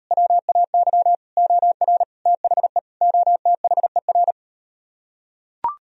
Morse code, heard as a keyed tone, about 38 wpm, 700 Hz. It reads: WAY OR THE OTHER